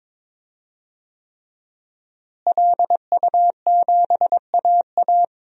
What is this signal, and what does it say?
Morse code, heard as a keyed tone, about 22 wpm, 700 Hz. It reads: LU7AA
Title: DXCC-100 Call Signs - 5 Characters - Repeated 3x - 22wpm